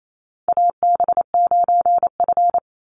Morse code, heard as a keyed tone, about 28 wpm, 700 Hz. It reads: AB9F
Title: USA Call Signs 28wpm